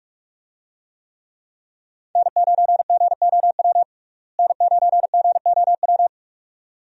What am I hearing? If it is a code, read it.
Morse code, heard as a keyed tone, about 46 wpm, 700 Hz. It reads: N9GOW N9GOW
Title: USA Call Signs - 5 Characters - Repeated 2x - 45wpm